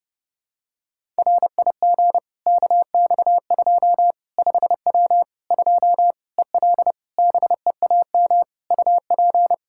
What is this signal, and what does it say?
Morse code, heard as a keyed tone, about 30 wpm, 700 Hz. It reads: RIG KX2 5W 2 EL BEAM UP
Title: Practice QSO - Exchange 2 - Info - 30wpm